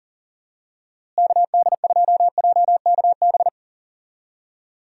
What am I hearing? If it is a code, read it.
Morse code, heard as a keyed tone, about 38 wpm, 700 Hz. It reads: KD2JKB
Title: USA Call Signs - 6 Characters - Repeated 2x - 40wpm